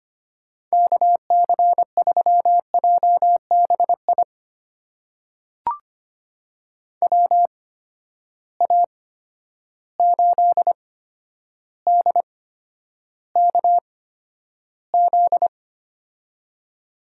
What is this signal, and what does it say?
Morse code, heard as a keyed tone, about 25 wpm, spaced as for 11 wpm, 700 Hz. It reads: KC3JBI WA8DKZ
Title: USA Call Signs - 6 Characters - 8x Character Spacing - 25wpm